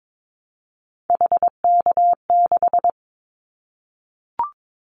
Morse code, heard as a keyed tone, about 22 wpm, 700 Hz. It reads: HK6
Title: Callsign Prefixes for Top 100 DXCC - 22wpm